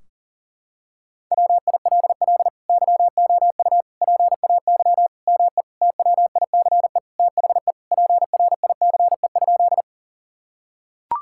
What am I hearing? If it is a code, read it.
Morse code, heard as a keyed tone, about 40 wpm, 700 Hz. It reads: WILL YOU PAY ME TWICE THE PRICE?
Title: Sentences from Top 1600 Words 40wpm